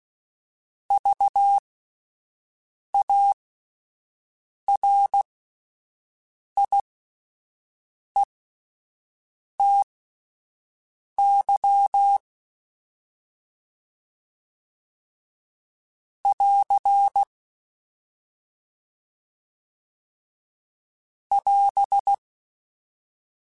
Morse code, heard as a keyed tone, about 16 wpm, spaced as for 5 wpm, 800 Hz. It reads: VARIETY <AR> <AS>